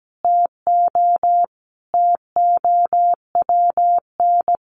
Morse code, heard as a keyed tone, about 17 wpm, 700 Hz. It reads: TO TOWN
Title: Sentences from Top 1400 Words 17wpm